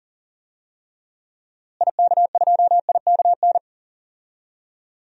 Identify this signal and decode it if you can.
Morse code, heard as a keyed tone, about 40 wpm, 700 Hz. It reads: IK2IKN
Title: DXCC-100 Call Signs - 6 Characters - Repeated 3x - 40wpm